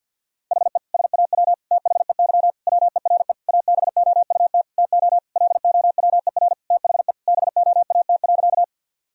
Morse code, heard as a keyed tone, about 50 wpm, 700 Hz. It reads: HE SAW THEY WERE ABOUT TO LOWER THE BOAT.
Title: Sentences from Top 1000 Words - Review Part 4 - 50wpm